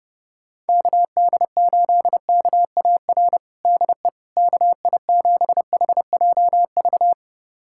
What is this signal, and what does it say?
Morse code, heard as a keyed tone, about 30 wpm, 700 Hz. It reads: KD8KAR DE KI7HJV
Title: USA Callsign DE Callsign - 6 Characters - 30wpm